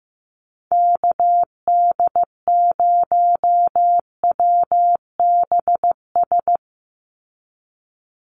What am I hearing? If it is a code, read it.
Morse code, heard as a keyed tone, about 15 wpm, 700 Hz. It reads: KD0WBS
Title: USA Call Signs 15wpm